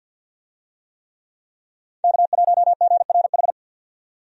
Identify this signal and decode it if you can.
Morse code, heard as a keyed tone, about 49 wpm, 700 Hz. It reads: K1GRH
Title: USA Call Signs - 5 Characters - 50wpm